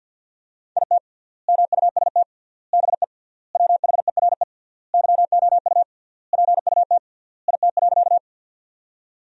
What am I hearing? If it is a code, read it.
Morse code, heard as a keyed tone, about 50 wpm, 700 Hz. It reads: IT MUST BE WHERE YOU PUT IT.
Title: 3x Word-Spacing - Sentences from Top 200 Words - 50wpm